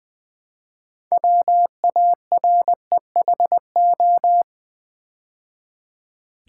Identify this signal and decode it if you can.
Morse code, heard as a keyed tone, about 20 wpm, 700 Hz. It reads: WAREHO
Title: Build A Word - 3 Letters At A Time - 9-Letter Words - 20wpm